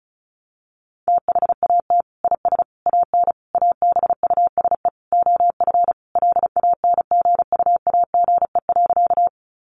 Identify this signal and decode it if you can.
Morse code, heard as a keyed tone, about 34 wpm, 700 Hz. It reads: THAT IS AN ABUSE OF LANGUAGE.
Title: Sentences from Top 1300 Words 35wpm